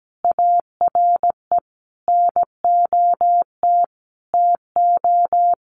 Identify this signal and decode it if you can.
Morse code, heard as a keyed tone, about 17 wpm, 700 Hz. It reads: ARE NOT TO
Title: Sentences from Top 1200 Words 17wpm